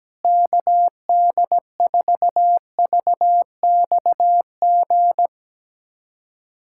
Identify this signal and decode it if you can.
Morse code, heard as a keyed tone, about 17 wpm, 700 Hz. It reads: KD4VXG
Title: USA Callsign DE Callsign 17wpm